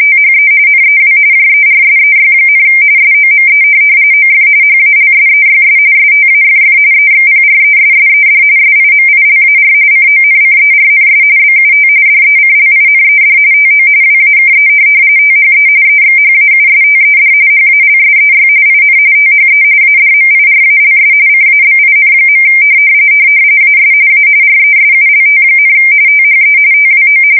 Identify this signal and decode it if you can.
RTTY, about 45.45 baud. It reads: THE NEW JERSEY, SOME OF HALSEYS STAFF EXPRESSED SIMILAR CONCERNS BUT DECLINED TO WAKE THE EXHAUSTED HALSEY IN THE ADMIRALS CABIN. AS A RESULT, THE TWO
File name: fsk170 / USS-New-Jersey